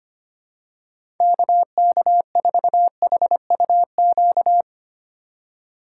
Morse code, heard as a keyed tone, about 25 wpm, 690 Hz. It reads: KK4HUQ
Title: USA Call Signs - 6 Characters - 3x Character Spacing - 25wpm